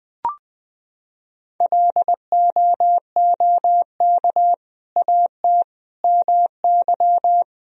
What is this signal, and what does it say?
Morse code, heard as a keyed tone, about 20 wpm, 700 Hz. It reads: LOOK AT MY
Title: Sentences from Top 1000 Words - Review Part 1 - 20wpm